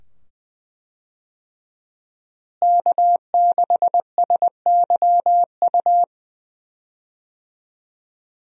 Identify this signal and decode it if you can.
Morse code, heard as a keyed tone, about 20 wpm, 700 Hz. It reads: K6SYU
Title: USA Call Signs - 5 Characters - Repeated 2x - 20wpm